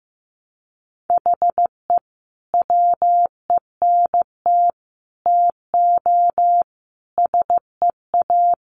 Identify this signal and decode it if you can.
Morse code, heard as a keyed tone, about 15 wpm, 700 Hz. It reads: HE WENT TO SEA
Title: Sentences from Top 1200 Words 15wpm